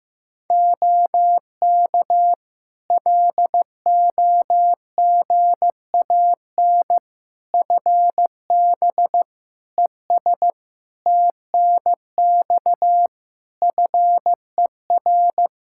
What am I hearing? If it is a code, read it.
Morse code, heard as a keyed tone, about 14 wpm, 700 Hz. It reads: OK LOGAN FB ES TNX FER
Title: Practice QSO - Exchange 2 - Info - 15wpm